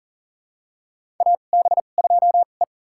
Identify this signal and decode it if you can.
Morse code, heard as a keyed tone, about 40 wpm, spaced as for 29 wpm, 700 Hz. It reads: AB2E
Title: USA Call Signs - 4 Characters - 2x Character Spacing - 40wpm